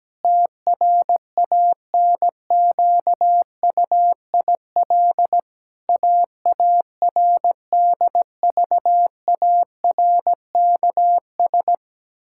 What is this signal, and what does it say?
Morse code, heard as a keyed tone, about 17 wpm, 700 Hz. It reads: TRANQUIL AARDVARKS
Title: Pangram Challenge v2 - 17wpm